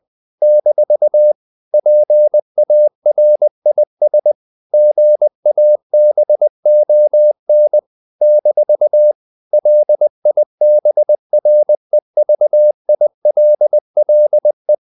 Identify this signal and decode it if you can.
Morse code, heard as a keyed tone, about 20 wpm, 600 Hz. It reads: - PARIS GABON - LIBREVILLE
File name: CW / Practice / NationalCapitals20